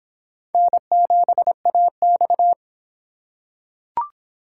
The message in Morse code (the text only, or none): N7AX